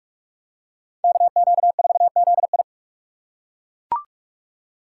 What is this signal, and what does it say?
Morse code, heard as a keyed tone, about 40 wpm, 700 Hz. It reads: KQ4ZI